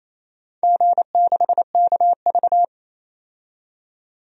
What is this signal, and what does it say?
Morse code, heard as a keyed tone, about 28 wpm, 700 Hz. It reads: G6KV
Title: DXCC-100 Call Signs - 4 Characters - 28wpm